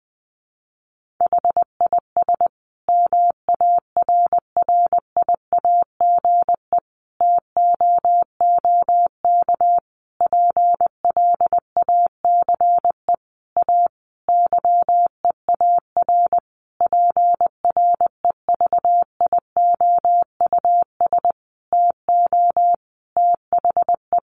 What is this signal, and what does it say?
Morse code, heard as a keyed tone, about 20 wpm, 700 Hz. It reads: HIS MARRIAGE TOOK PLACE A YEAR PREVIOUS TO THE
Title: Sentences from Top 1500 Words 20wpm